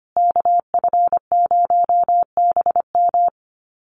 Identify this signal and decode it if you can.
Morse code, heard as a keyed tone, about 25 wpm, 700 Hz. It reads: KF0BM
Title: USA Call Signs Repeated 2x 25wpm